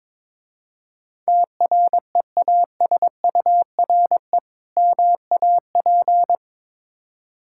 Morse code, heard as a keyed tone, about 22 wpm, 690 Hz. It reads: TREASURE MAP
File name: Difficult Words with Hints - 22wpm